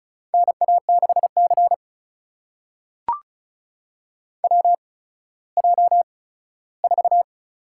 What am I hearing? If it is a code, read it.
Morse code, heard as a keyed tone, about 36 wpm, spaced as for 17 wpm, 700 Hz. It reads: NA6C WJ4
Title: USA Call Signs - 8x Character Spacing - 35wpm